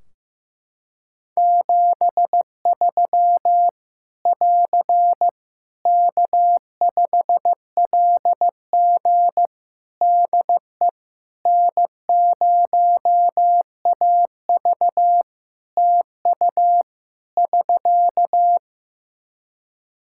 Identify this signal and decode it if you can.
Morse code, heard as a keyed tone, about 15 wpm, 700 Hz. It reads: 73 <AR> K5LG DE N0AV TU <SK>
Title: K1USN SST QSO Practice 15wpm